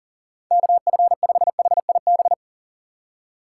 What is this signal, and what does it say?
Morse code, heard as a keyed tone, about 37 wpm, 700 Hz. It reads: KF5HIB
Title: USA Call Signs - 2x Character Spacing - 40wpm